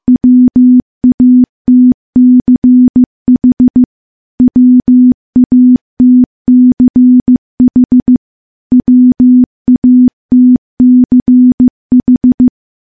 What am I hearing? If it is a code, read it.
Morse code, heard as a keyed tone, about 15 wpm, 250 Hz. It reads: WATCH WATCH WATCH
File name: morse-training-15WPM